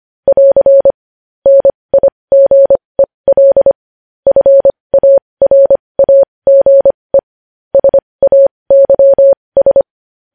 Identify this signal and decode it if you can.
Morse code, heard as a keyed tone, about 25 wpm, spaced as for 20 wpm, 550 Hz. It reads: <AR> NIGEL FARAGE SAYS
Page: news-headlines-in-morse-code-at-20-wpm